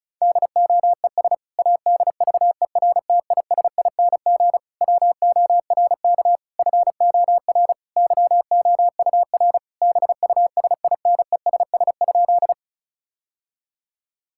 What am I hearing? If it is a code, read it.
Morse code, heard as a keyed tone, about 36 wpm, 690 Hz. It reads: DOES ADVERTISING WORK FOR YOUR BUSINESS?
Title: Sentences from Top 1000 Words - Review Part 6 - 35wpm